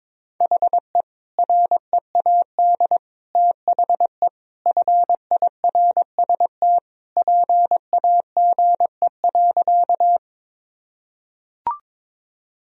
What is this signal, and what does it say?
Morse code, heard as a keyed tone, about 21 wpm, 700 Hz. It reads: HE READ THE FIRST PAGE.
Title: Sentences from Top 1000 Words - Review Part 1 - 22wpm